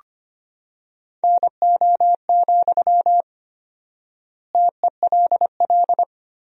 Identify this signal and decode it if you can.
Morse code, heard as a keyed tone, about 25 wpm, 700 Hz. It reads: NO, TELL
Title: 4x Word-Spacing - Sentences from Top 200 Words - 25wpm